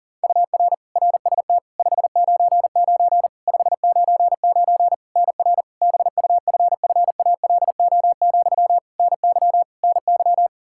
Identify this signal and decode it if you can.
Morse code, heard as a keyed tone, about 37 wpm, 700 Hz. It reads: UR RST 599 599 NR BUFFALO, NY NY
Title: K1USN SST QSO Practice 40wpm